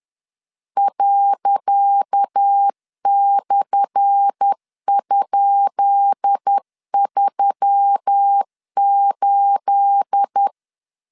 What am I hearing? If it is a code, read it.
Morse code, heard as a keyed tone, about 10 wpm, 800 Hz. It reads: ./?38
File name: LEZIONI / 10 SIGNS / 10C DOT 10wpm 2940